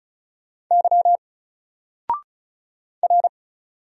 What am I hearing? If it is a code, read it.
Morse code, heard as a keyed tone, about 33 wpm, 700 Hz. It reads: Y R
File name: Single-Letter-Number-35wpm